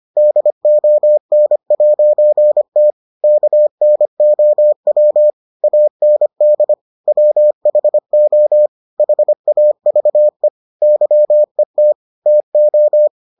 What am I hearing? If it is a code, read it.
Morse code, heard as a keyed tone, about 25 wpm, 600 Hz. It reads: DON'T KNOW AND WHO HAVE YET TO